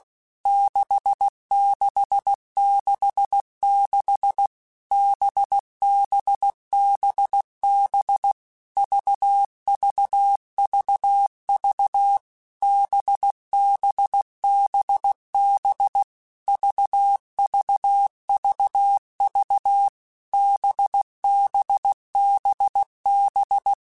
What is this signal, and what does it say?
Morse code, heard as a keyed tone, about 16 wpm, 800 Hz. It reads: 6666 BBBB VVVV BBBB VVVV BBBB
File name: Basic Letter Groups By7Groups / VB46 / VB46@15